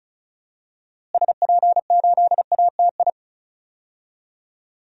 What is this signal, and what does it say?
Morse code, heard as a keyed tone, about 36 wpm, 700 Hz. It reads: SP8ATI
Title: DXCC-100 Call Signs - 6 Characters - Repeated 3x - 35wpm